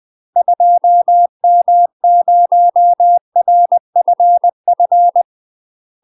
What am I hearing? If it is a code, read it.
Morse code, heard as a keyed tone, about 20 wpm, 700 Hz. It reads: 2M0RFF
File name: UK MASTER QSOs TRAINING / 20WPM / UK MASTER QSOs 0011